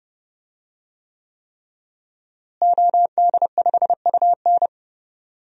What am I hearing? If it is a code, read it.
Morse code, heard as a keyed tone, about 30 wpm, 700 Hz. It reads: OD5UN
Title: DXCC-100 Call Signs - 5 Characters - Repeated 3x - 30wpm